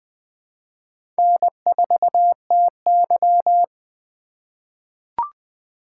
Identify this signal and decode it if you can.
Morse code, heard as a keyed tone, about 20 wpm, 700 Hz. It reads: N4TY
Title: USA Call Signs - 4 Characters - 2x Character Spacing - 20wpm